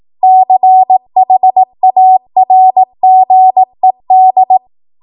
Morse code, heard as a keyed tone, about 18 wpm, 750 Hz. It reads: CHARGED
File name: Morse / 18WPM